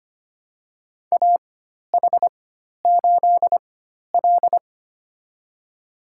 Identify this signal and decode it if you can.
Morse code, heard as a keyed tone, about 25 wpm, spaced as for 12 wpm, 700 Hz. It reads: AH8L
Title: USA Call Signs - 4 Characters - 4x Character Spacing - 25wpm